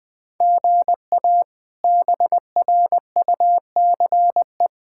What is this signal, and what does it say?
Morse code, heard as a keyed tone, about 21 wpm, 700 Hz. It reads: GA BRUCE
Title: K1USN SST QSO Practice - 20wpm